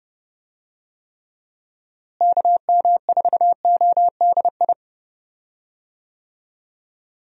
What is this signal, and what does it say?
Morse code, heard as a keyed tone, about 30 wpm, 700 Hz. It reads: KM4ODI
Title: USA Call Signs - 6 Characters - Repeated 2x - 30wpm